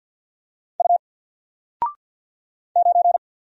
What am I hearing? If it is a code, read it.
Morse code, heard as a keyed tone, about 48 wpm, 700 Hz. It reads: U 9
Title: Single Letter-Number 50wpm